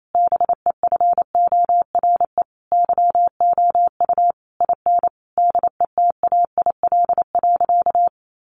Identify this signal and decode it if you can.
Morse code, heard as a keyed tone, about 28 wpm, 700 Hz. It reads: BEFORE YOU IN DETAIL.
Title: Sentences from Top 1500 Words 28wpm